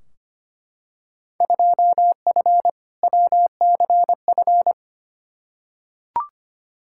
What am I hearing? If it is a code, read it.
Morse code, heard as a keyed tone, about 25 wpm, 700 Hz. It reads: 2F WCF
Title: Field Day Exchanges Encore - 25wpm